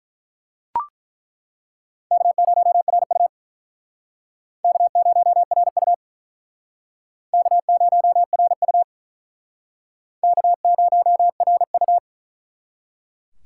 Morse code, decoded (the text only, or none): K0RU K0RU K0RU K0RU